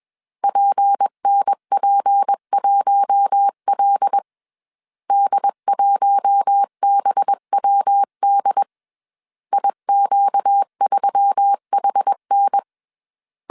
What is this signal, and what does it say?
Morse code, heard as a keyed tone, about 21 wpm, 800 Hz. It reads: PNP1L D1BWD IQ3HN
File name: CW / DET / 1 6 1500C S C 20wpm 2116